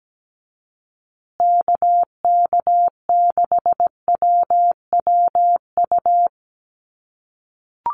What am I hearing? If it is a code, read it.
Morse code, heard as a keyed tone, about 17 wpm, 700 Hz. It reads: KK6WWU
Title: USA Call Signs Repeated 3x 17wpm